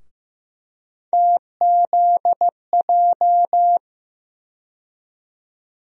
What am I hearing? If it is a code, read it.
Morse code, heard as a keyed tone, about 15 wpm, 700 Hz. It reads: TZJ
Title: USA Call Sign Postfixes - 3 Letters 15wpm